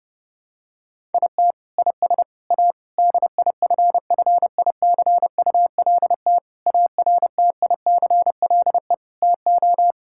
Morse code, heard as a keyed tone, about 30 wpm, 700 Hz. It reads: IT IS A DIFFICULT ARTICLE TO
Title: Sentences from Top 1000 Words - Review Part 5 - 30wpm